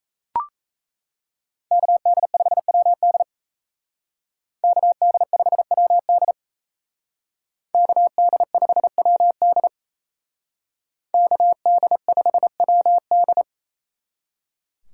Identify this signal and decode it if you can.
Morse code, heard as a keyed tone, about 36 wpm, 700 Hz. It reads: KD5WD KD5WD KD5WD KD5WD